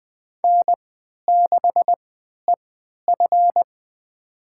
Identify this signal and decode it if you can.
Morse code, heard as a keyed tone, about 20 wpm, spaced as for 11 wpm, 700 Hz. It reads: N6EF
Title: USA Call Signs - 4 Characters - 3x Character Spacing - 20wpm